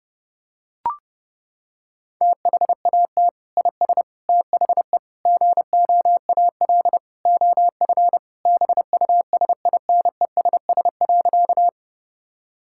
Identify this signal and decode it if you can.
Morse code, heard as a keyed tone, about 29 wpm, 700 Hz. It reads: THAT IS THE GOAL OF BUSINESS.
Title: Sentences from Top 1000 Words - Review Part 5 - 30wpm